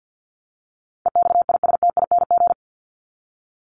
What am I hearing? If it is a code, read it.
Morse code, heard as a keyed tone, about 50 wpm, 700 Hz. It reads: EXISTING
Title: Top 1251-1500 Words 50wpm